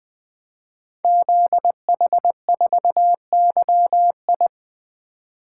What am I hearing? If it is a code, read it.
Morse code, heard as a keyed tone, about 20 wpm, 700 Hz. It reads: ZH4YI
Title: Code Groups - 5 Letter:Numbers - 20wpm